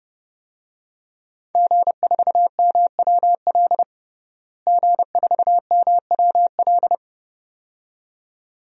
Morse code, heard as a keyed tone, about 32 wpm, 700 Hz. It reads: G4MWL G4MWL
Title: DXCC-100 Call Signs - 5 Characters - Repeated 2x - 30wpm